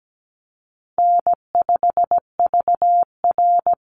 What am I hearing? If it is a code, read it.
Morse code, heard as a keyed tone, about 17 wpm, 700 Hz. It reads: N5VR
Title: Easy Call Signs Repeated 2x - 17wpm